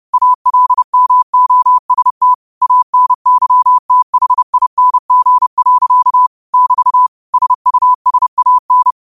Morse code, heard as a keyed tone, about 30 wpm, 1000 Hz. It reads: ALMOST ANYTHING. <BT> SUSAN
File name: qotdfiles / 30wpm